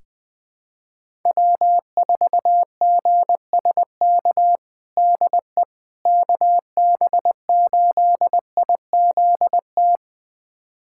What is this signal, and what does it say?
Morse code, heard as a keyed tone, about 21 wpm, 700 Hz. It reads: W4GSK DE KB8IZT
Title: USA Callsign DE Callsign 20wpm